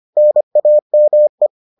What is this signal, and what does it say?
Morse code, heard as a keyed tone, about 25 wpm, 600 Hz. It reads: NAME